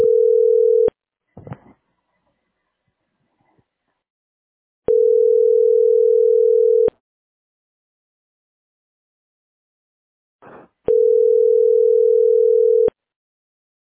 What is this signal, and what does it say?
Ringback tone (North America), 440 + 480 Hz.